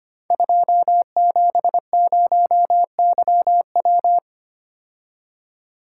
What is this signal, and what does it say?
Morse code, heard as a keyed tone, about 25 wpm, 700 Hz. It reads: 270YW
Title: Code Groups - 5 Letter:Numbers - 25wpm